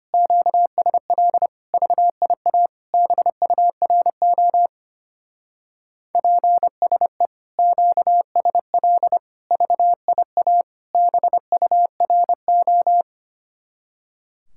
Morse code, heard as a keyed tone, about 32 wpm, 700 Hz. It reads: QSL VIA BURO PSE QSL VIA BURO